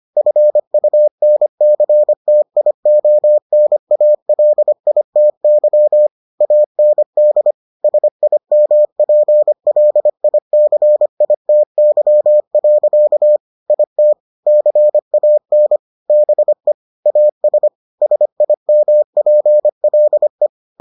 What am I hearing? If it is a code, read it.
Morse code, heard as a keyed tone, about 25 wpm, 600 Hz. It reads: FUNCTIONALITY AND SIMPLICITY. IT CAN BE AS SIMPLE